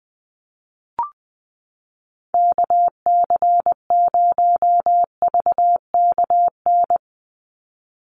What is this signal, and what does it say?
Morse code, heard as a keyed tone, about 20 wpm, 700 Hz. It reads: KC0VKN
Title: Call Signs Repeated 3x - 20wpm